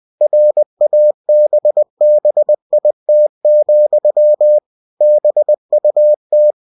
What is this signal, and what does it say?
Morse code, heard as a keyed tone, about 20 wpm, 600 Hz. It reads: RABBIT, BUT
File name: cw-books / winnie-the-pooh / winnie-20wpm-page-0084